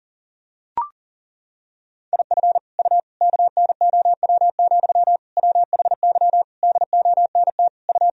Morse code, heard as a keyed tone, about 37 wpm, 700 Hz. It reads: IF U KNOW, WHY DONT U
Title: Sentences from Top 300 Words and Abbreviations 40wpm